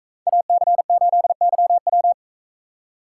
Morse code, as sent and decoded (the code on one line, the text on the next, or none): .- -.-. ---.. -.-- .--
AC8YW